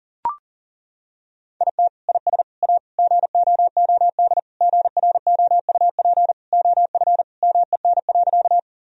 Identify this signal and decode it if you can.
Morse code, heard as a keyed tone, about 40 wpm, 700 Hz. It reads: IT IS A GOOD GROUP OF MEN.
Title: Sentences from Top 1000 Words - Review Part 1 - 40wpm